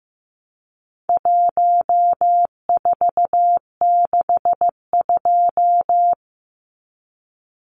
Morse code, heard as a keyed tone, about 15 wpm, 700 Hz. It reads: 1462
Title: Numbers - 4-Digit 15wpm